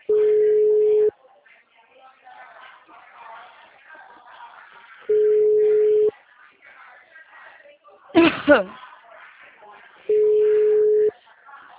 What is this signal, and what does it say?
Ringback tone (Europe, 425 Hz), with speech.
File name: out- -2011- - - .6715